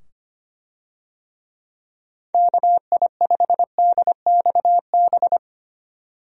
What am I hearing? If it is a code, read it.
Morse code, heard as a keyed tone, about 25 wpm, 700 Hz. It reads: KI5DXB
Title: USA Call Signs - 6 Characters - 25wpm